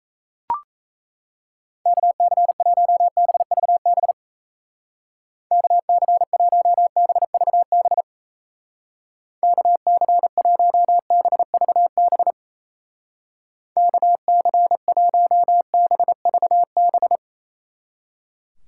Morse code, heard as a keyed tone, about 36 wpm, 700 Hz. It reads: KC1BVB KC1BVB KC1BVB KC1BVB